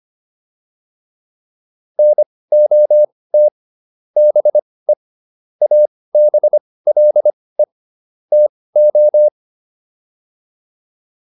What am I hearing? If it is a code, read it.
Morse code, heard as a keyed tone, about 25 wpm, spaced as for 18 wpm, 600 Hz. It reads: NOT BE ABLE TO